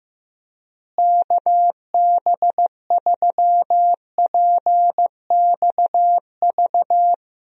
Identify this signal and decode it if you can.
Morse code, heard as a keyed tone, about 15 wpm, 700 Hz. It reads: KB3PXV
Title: USA Call Signs - 6 Characters - Repeated 3x - 15wpm